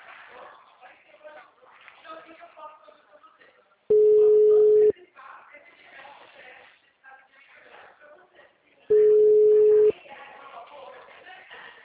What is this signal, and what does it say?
Ringback tone (Europe, 425 Hz), with speech.